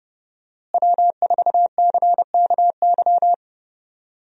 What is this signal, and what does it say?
Morse code, heard as a keyed tone, about 32 wpm, 700 Hz. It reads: W4CKY
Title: USA Call Signs - 5 Characters - Repeated 3x - 30wpm